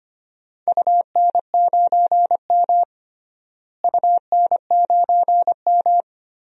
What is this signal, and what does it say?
Morse code, heard as a keyed tone, about 25 wpm, 700 Hz. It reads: UN9M UN9M
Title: DXCC-100 Call Signs - 4 Characters - Repeated 2x - 25wpm